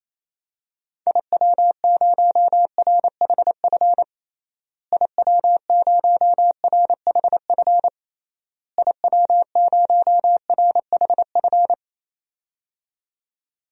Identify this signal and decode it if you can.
Morse code, heard as a keyed tone, about 28 wpm, 700 Hz. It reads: IW0RHF IW0RHF IW0RHF
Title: DXCC-100 Call Signs - 6 Characters - Repeated 3x - 28wpm